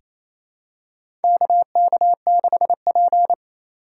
Morse code, heard as a keyed tone, about 28 wpm, 700 Hz. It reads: KK6P
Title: USA Call Signs - 4 Characters - 6x Character Spacing - 28wpm